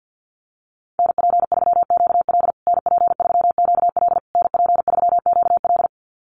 Morse code, heard as a keyed tone, about 50 wpm, 700 Hz. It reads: NP3QL NP3QL NP3QL
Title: USA Call Signs Repeated 3x 50wpm